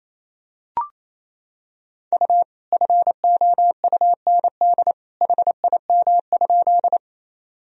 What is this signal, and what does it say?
Morse code, heard as a keyed tone, about 28 wpm, 700 Hz. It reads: U FOUND HIM?
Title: Sentences from Top 300 Words and Abbreviations 28wpm